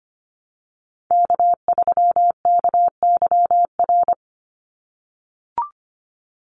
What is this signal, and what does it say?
Morse code, heard as a keyed tone, about 25 wpm, 700 Hz. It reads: K3KYR
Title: NA SOTA Chaser Callsigns 25wpm